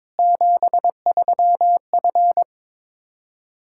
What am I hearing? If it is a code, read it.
Morse code, heard as a keyed tone, about 22 wpm, 700 Hz. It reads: 73F
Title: Practice QSO - Exchange 2 - Info - 22wpm